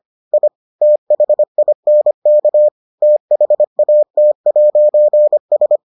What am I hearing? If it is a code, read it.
Morse code, heard as a keyed tone, about 25 wpm, 600 Hz. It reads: I THINK THAT'S